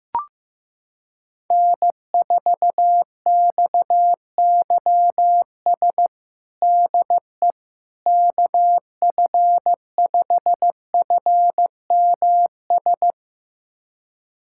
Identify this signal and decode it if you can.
Morse code, heard as a keyed tone, about 15 wpm, 700 Hz. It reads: N4XYS DE KF5FMS
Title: USA Callsign DE Callsign 15wpm